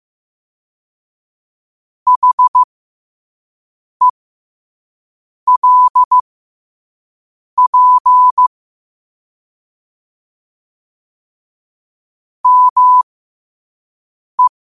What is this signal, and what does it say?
Morse code, heard as a keyed tone, about 15 wpm, spaced as for 5 wpm, 1000 Hz. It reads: HELP ME
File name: qotdfiles / 5wpm